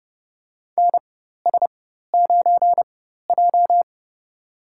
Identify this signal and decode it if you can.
Morse code, heard as a keyed tone, about 30 wpm, spaced as for 14 wpm, 700 Hz. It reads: NS9J
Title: USA Call Signs - 4 Characters - 4x Character Spacing - 30wpm